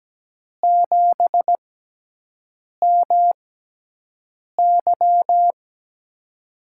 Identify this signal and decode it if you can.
Morse code, heard as a keyed tone, about 17 wpm, spaced as for 6 wpm, 700 Hz. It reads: 7MY
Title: USA Call Signs - 6x Character Spacing - 17wpm